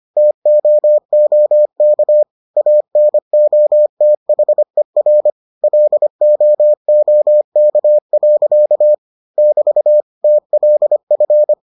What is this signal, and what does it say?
Morse code, heard as a keyed tone, about 25 wpm, 600 Hz. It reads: TOOK ANOTHER LOOK. <BT> TLF